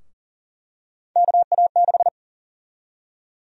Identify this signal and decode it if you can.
Morse code, heard as a keyed tone, about 40 wpm, 700 Hz. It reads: KA6
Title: USA Call Sign Prefixes - Repeated 2x - 40wpm